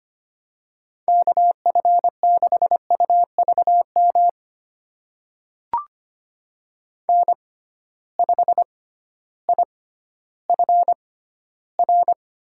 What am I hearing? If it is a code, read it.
Morse code, heard as a keyed tone, about 25 wpm, spaced as for 13 wpm, 700 Hz. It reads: KF6UVM N5IFR
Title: USA Call Signs - 6x Character Spacing - 25wpm